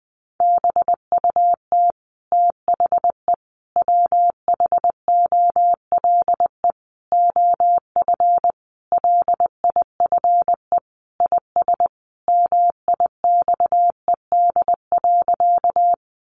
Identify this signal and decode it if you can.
Morse code, heard as a keyed tone, about 20 wpm, 700 Hz. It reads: BUT THE WHOLE OF LIFE IS MIXED.
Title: Sentences from Top 1500 Words 20wpm